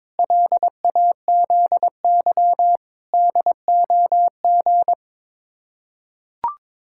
Morse code, heard as a keyed tone, about 21 wpm, 700 Hz. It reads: LAZY DOG
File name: practice / 22wpm / Sets of 8 Words - 22wpm